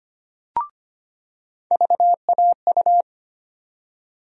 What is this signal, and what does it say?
Morse code, heard as a keyed tone, about 25 wpm, 700 Hz. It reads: VAU
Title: USA Call Sign Postfixes - 3 Letters 25wpm